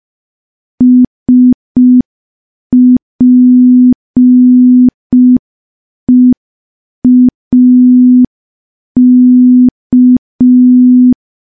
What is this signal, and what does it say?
Morse code, heard as a keyed tone, about 5 wpm, 250 Hz. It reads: SPEAK
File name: morse / morse-training-05WPM